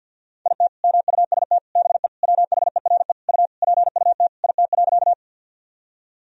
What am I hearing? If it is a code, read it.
Morse code, heard as a keyed tone, about 50 wpm, 700 Hz. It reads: IT MUST BE WHERE U PUT IT.
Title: Sentences from Top 200 Words and Abbreviations 50wpm